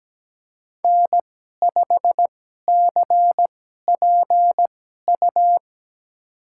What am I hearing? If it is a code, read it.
Morse code, heard as a keyed tone, about 17 wpm, spaced as for 12 wpm, 700 Hz. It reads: N5CPU
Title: USA Call Signs - 5 Characters - 2x Character Spacing - 17wpm